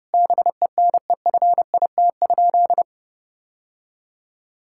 Morse code, heard as a keyed tone, about 32 wpm, 700 Hz. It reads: BENEFIT?
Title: Sentences from Top 1000 Words - Review Part 6 - 30wpm